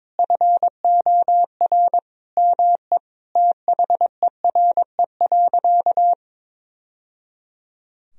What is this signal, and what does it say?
Morse code, heard as a keyed tone, about 22 wpm, 700 Hz. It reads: FOR ME THERE.
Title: Sentences from Top 1000 Words - Review Part 4 - 22wpm